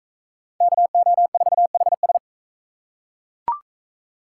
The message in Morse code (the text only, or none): KO4HS